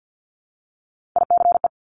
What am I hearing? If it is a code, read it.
Morse code, heard as a keyed tone, about 50 wpm, 700 Hz. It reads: ICE
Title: Multi-letter Phonemes Repeated 3x 50wpm